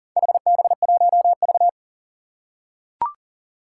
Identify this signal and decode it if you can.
Morse code, heard as a keyed tone, about 38 wpm, 700 Hz. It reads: HB1V
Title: Four Challenging Letter-Number Combinations 40wpm